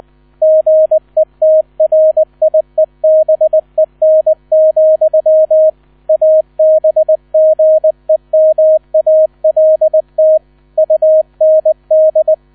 Morse code, heard as a keyed tone, about 20 wpm, 630 Hz, with radio noise. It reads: GETRIEBEN, ABGEMALT UND
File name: 000 Vorwort 20wpm